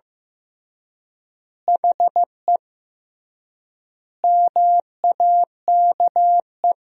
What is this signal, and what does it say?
Morse code, heard as a keyed tone, about 15 wpm, 700 Hz. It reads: HE MAKE
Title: 3x Word-Spacing - Sentences from Top 400 Words - 15wpm